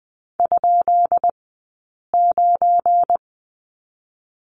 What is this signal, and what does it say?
Morse code, heard as a keyed tone, about 20 wpm, spaced as for 8 wpm, 690 Hz. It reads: ?9
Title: 197 Lesson 32d - Letter k - Two Characters 20wpm